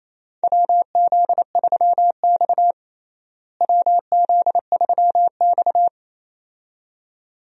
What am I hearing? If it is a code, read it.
Morse code, heard as a keyed tone, about 27 wpm, 700 Hz. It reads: WZ3X WZ3X
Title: USA Call Signs - 4 Characters - Repeated 2x - 28wpm